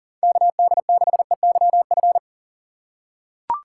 Morse code, heard as a keyed tone, about 40 wpm, 700 Hz. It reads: KD6EYF
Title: USA Call Signs - 6 Characters - 4x Character Spacing - 40wpm